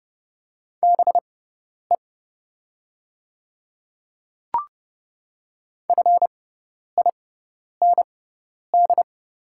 Morse code, heard as a keyed tone, about 30 wpm, spaced as for 10 wpm, 700 Hz. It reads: BE FIND
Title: 6x Character Spacing - Top 200 Words - 30wpm